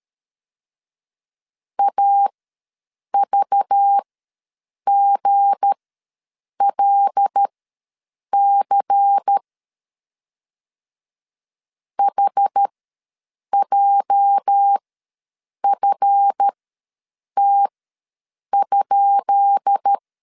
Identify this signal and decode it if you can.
Morse code, heard as a keyed tone, about 13 wpm, spaced as for 7 wpm, 800 Hz. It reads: AVGLC HJFT?
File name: LEZIONI / 10 SIGNS / 10B INT ws 12wpm 3527